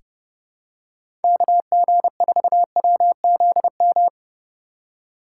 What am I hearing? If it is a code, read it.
Morse code, heard as a keyed tone, about 30 wpm, 700 Hz. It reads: KG4WZM
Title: USA Call Signs - 4x Character Spacing - 30wpm